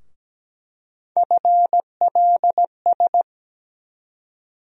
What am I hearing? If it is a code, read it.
Morse code, heard as a keyed tone, about 17 wpm, 700 Hz. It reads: FLS
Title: USA Call Sign Postfixes - 3 Letters 17wpm